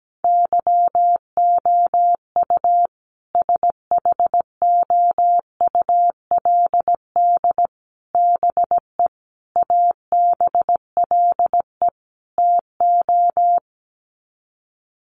Morse code, heard as a keyed tone, about 17 wpm, 700 Hz. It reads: YOU SHOULD BE ABLE TO
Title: Sets of 5 Words Encore 17wpm